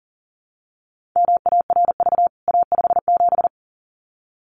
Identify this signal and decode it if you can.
Morse code, heard as a keyed tone, about 40 wpm, 700 Hz. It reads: MARV A57
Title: CWT Exchange 40wpm